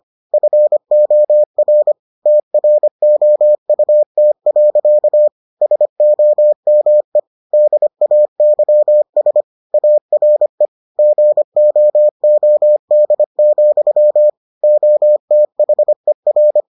Morse code, heard as a keyed tone, about 25 wpm, 600 Hz. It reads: FOR TROUT. SOME DAYS ARE GOOD, OTHER